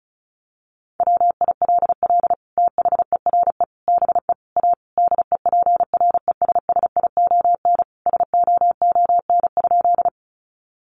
Morse code, heard as a keyed tone, about 34 wpm, 700 Hz. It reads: WILL THERE BE A DEPRESSION SOON?
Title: Sentences from Top 1300 Words 35wpm